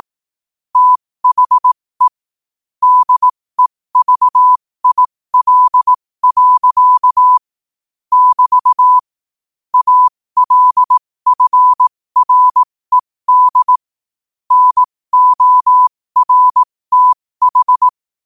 Morse code, heard as a keyed tone, about 18 wpm, spaced as for 15 wpm, 1000 Hz. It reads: THE DEVIL. <BT> ALFRED NORTH
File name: qotdfiles / 15wpm